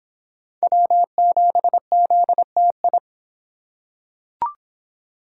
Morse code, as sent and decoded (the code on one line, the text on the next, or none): .-- --... --.. - ..
W7ZTI